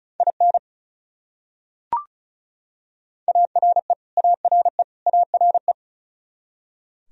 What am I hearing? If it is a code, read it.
Morse code, heard as a keyed tone, about 34 wpm, 700 Hz. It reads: IN ARE ARE ARE
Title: Top 100 Words Repeated 35wpm